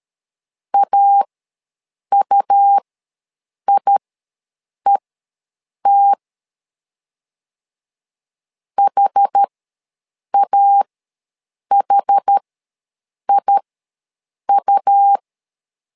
Morse code, heard as a keyed tone, about 13 wpm, spaced as for 7 wpm, 800 Hz. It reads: AUIET HAHIU
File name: LEZIONI / 2 UHDN / 2 UH ws 12wpm 1730